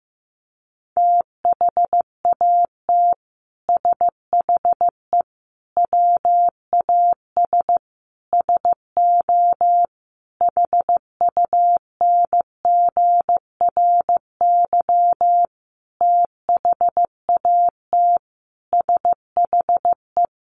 Morse code, heard as a keyed tone, about 15 wpm, 700 Hz. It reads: THAT SHE WAS SO HUNGRY THAT SHE
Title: Sentences from Top 1400 Words 15wpm